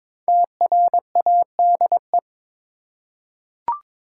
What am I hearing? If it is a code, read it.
Morse code, heard as a keyed tone, about 22 wpm, 700 Hz. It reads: TRADE
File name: 5 Letter Words - First 3 Letters Given - 22wpm